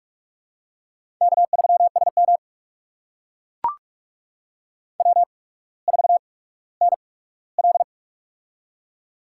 Morse code, heard as a keyed tone, about 45 wpm, spaced as for 19 wpm, 700 Hz. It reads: K3SM W4NL
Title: USA Call Signs - 8x Character Spacing - 45wpm